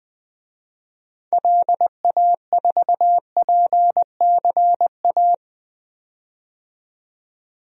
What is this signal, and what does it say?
Morse code, heard as a keyed tone, about 20 wpm, 700 Hz. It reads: LA4PCA
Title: DXCC-100 Call Signs - 6 Characters - 20wpm